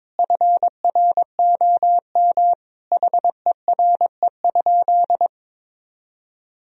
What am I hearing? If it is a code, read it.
Morse code, heard as a keyed tone, about 22 wpm, 700 Hz. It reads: FROM HERE?
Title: Sentences from Top 1000 Words - Review Part 4 - 22wpm